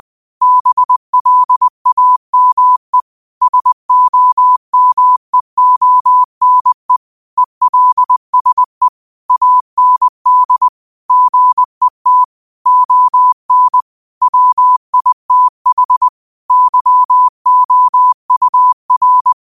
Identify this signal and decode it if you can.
Morse code, heard as a keyed tone, about 20 wpm, 1000 Hz. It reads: BLAME SOMEONE ELSE AND GET ON WITH YOUR
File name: qotdfiles / 20wpm